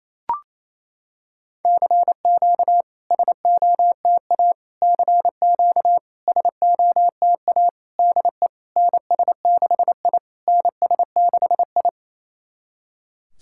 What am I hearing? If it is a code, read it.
Morse code, heard as a keyed tone, about 28 wpm, 700 Hz. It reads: CQ SOTA CQ SOTA DE NS6I NS6I
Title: SOTA QSO Practice 28wpm